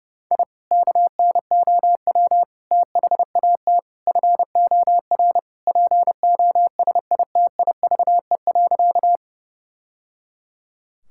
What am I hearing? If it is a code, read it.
Morse code, heard as a keyed tone, about 32 wpm, 700 Hz. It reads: I KNOW THAT FOR POSITIVE.
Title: Sentences from Top 1000 Words - Review Part 5 - 30wpm